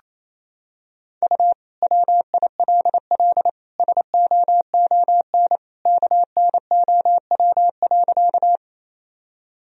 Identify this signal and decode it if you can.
Morse code, heard as a keyed tone, about 28 wpm, 700 Hz. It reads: U WILL SOON KNOW.
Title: Sentences from Top 400 Words and Abbreviations 28wpm